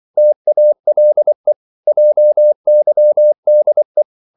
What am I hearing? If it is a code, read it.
Morse code, heard as a keyed tone, about 24 wpm, 600 Hz. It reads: TALE JYDE
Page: Korte danske ord 24wpm